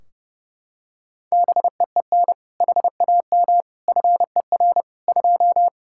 Morse code, heard as a keyed tone, about 32 wpm, 700 Hz. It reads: BEEN HAM FER 2
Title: Practice QSO - Exchange 3 - More Info - 30wpm